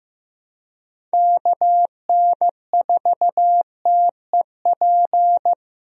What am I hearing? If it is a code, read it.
Morse code, heard as a keyed tone, about 16 wpm, 700 Hz. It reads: KN4TEP
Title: USA Call Signs - 6 Characters - Repeated 3x - 15wpm